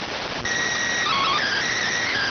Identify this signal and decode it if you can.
SSTV (slow-scan television), PD 290.